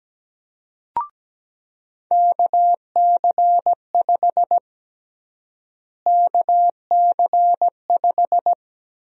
Morse code, heard as a keyed tone, about 17 wpm, 700 Hz. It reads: KC5 KC5
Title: USA Call Sign Prefixes - Repeated 2x - 17wpm